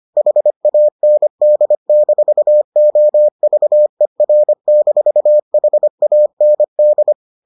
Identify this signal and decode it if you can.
Morse code, heard as a keyed tone, about 25 wpm, 600 Hz. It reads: HAND-OVER-HAND